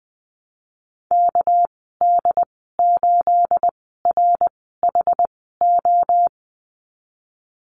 Morse code, heard as a keyed tone, about 20 wpm, spaced as for 15 wpm, 700 Hz. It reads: KD8RHO
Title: 213 Lesson 34g - Number 8 - Call Signs 20wpm